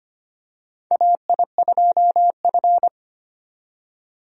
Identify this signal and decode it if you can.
Morse code, heard as a keyed tone, about 25 wpm, 700 Hz. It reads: AI2F
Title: USA Call Signs - 4 Characters - 25wpm